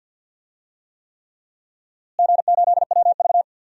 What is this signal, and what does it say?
Morse code, heard as a keyed tone, about 46 wpm, 700 Hz. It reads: G8WV
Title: DXCC-100 Call Signs - 4 Characters - Repeated 3x - 50wpm